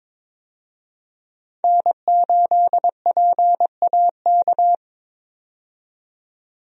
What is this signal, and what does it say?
Morse code, heard as a keyed tone, about 22 wpm, 700 Hz. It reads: N8PAK
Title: USA Call Signs - 5 Characters - 22wpm